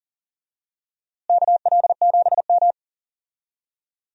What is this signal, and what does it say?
Morse code, heard as a keyed tone, about 40 wpm, 700 Hz. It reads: KL7M
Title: DXCC-100 Call Signs - 4 Characters - Repeated 2x - 40wpm